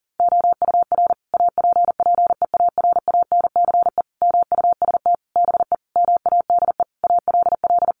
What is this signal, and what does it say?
Morse code, heard as a keyed tone, about 40 wpm, 700 Hz. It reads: OUR APPEARANCE MUST BE MADE ALL
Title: Sentences from Top 1100 Words 40wpm